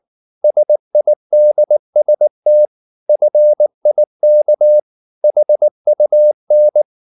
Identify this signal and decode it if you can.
Morse code, heard as a keyed tone, about 19 wpm, 600 Hz. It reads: SIDST FIK HUN